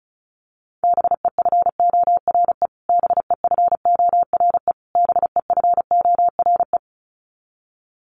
Morse code, heard as a keyed tone, about 34 wpm, 700 Hz. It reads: BEFORE BEFORE BEFORE
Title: Top 500 Words Repeated 35wpm